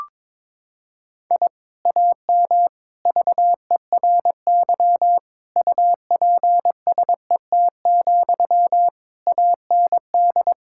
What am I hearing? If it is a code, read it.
Morse code, heard as a keyed tone, about 22 wpm, 700 Hz. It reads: I AM VERY UPSET, AND
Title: Sentences from Top 1000 Words - Review Part 5 - 22wpm